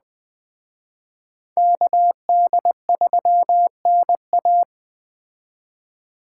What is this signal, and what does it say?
Morse code, heard as a keyed tone, about 20 wpm, 700 Hz. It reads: KD3NA
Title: Speed Racing - USA 2x2 Call Signs 17wpm